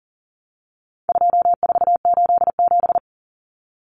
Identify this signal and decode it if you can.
Morse code, heard as a keyed tone, about 40 wpm, 700 Hz. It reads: 2487
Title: Numbers - 4-Digit 40wpm